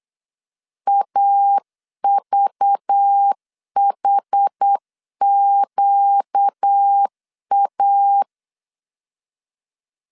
Morse code, heard as a keyed tone, about 8 wpm, 800 Hz. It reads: AVHQA
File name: CW / DET / 1 9 1500C S A 8wpm 5637